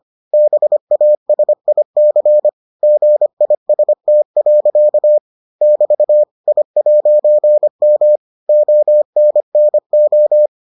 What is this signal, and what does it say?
Morse code, heard as a keyed tone, about 25 wpm, 600 Hz. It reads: BASIC GIST. <BT> I'M ONNO